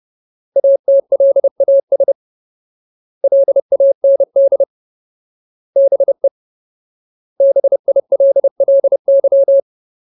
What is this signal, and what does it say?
Morse code, heard as a keyed tone, about 30 wpm, 550 Hz. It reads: ATLAS LAND BE BILLY
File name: Ultimate Word List / 30WPM Ultimate